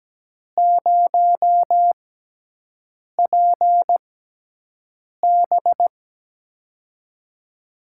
Morse code, heard as a keyed tone, about 17 wpm, spaced as for 6 wpm, 700 Hz. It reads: 0PB
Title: USA Call Signs - 4 Characters - 6x Character Spacing - 17wpm